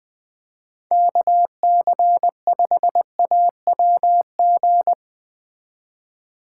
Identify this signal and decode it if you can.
Morse code, heard as a keyed tone, about 20 wpm, 700 Hz. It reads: KC5AWG
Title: Speed Racing - USA Call Signs 17wpm